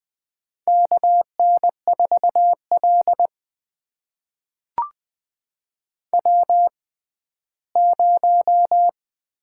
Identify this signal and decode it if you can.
Morse code, heard as a keyed tone, about 21 wpm, spaced as for 14 wpm, 700 Hz. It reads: KN4L W0
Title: USA Call Signs - 4 Characters - 6x Character Spacing - 20wpm